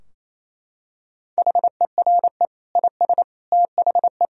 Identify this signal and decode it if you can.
Morse code, heard as a keyed tone, about 28 wpm, 700 Hz. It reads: HERE IS THE
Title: Sentences from Top 1000 Words - Review Part 4 - 28wpm